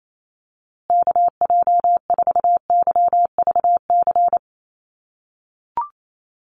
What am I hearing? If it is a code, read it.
Morse code, heard as a keyed tone, about 28 wpm, 700 Hz. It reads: KJ4YVC
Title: USA Call Signs Repeated 2x 28wpm